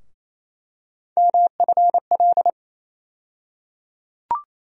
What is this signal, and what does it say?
Morse code, heard as a keyed tone, about 28 wpm, 700 Hz. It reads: MFL
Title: USA Call Sign Postfixes - 3 Letters 28wpm